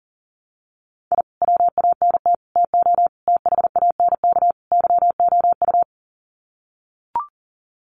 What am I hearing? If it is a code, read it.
Morse code, heard as a keyed tone, about 40 wpm, 700 Hz. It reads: I WANT TO THANK YOU
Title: Sets of 5 Words Encore 40wpm